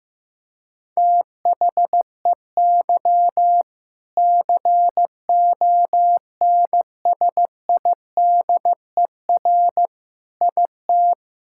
Morse code, heard as a keyed tone, about 15 wpm, 700 Hz. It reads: THEY CONSIDER IT
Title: Sentences from Top 1000 Words - Review Part 5 - 15wpm